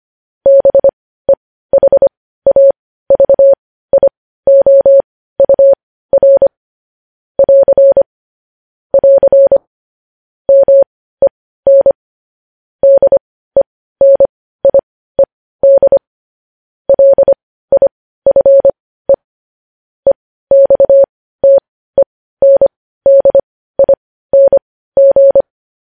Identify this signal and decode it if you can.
Morse code, heard as a keyed tone, about 25 wpm, spaced as for 15 wpm, 550 Hz. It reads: BEHAVIOUR <AR> <AR> MEN DENIED LIFE EXTENDING
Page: news-headlines-in-morse-code-at-15-wpm